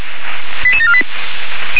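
Five-tone selective call (ZVEI), heard as quick runs of five tones, tones of about 70 ms. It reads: 88748
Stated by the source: Five-tone ZVEI selective call received on an SDR.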